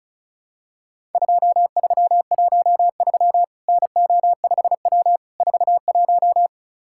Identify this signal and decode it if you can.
Morse code, heard as a keyed tone, about 34 wpm, 700 Hz. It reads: 2313 NO5W 41
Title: Worked All Europe DX Contest - QTC Exchange - 35wpm